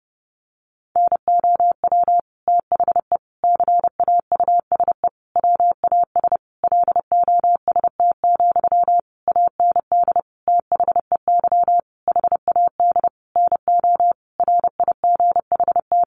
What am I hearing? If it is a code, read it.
Morse code, heard as a keyed tone, about 30 wpm, 700 Hz. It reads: NOW THE CAUSE WAS LOST, AND THEY HAD NO RIGHT
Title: Sentences from Top 1500 Words 30wpm